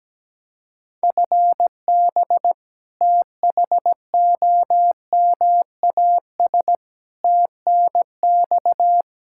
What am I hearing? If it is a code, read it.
Morse code, heard as a keyed tone, about 17 wpm, 700 Hz. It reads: FB THOMAS TNX
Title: Practice QSO - Exchange 3 - More Info - 17wpm